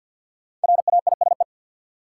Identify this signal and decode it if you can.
Morse code, heard as a keyed tone, about 53 wpm, 700 Hz. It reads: RAISE
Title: Sentences from Top 1400 Words - New Word Drill Alt - 50wpm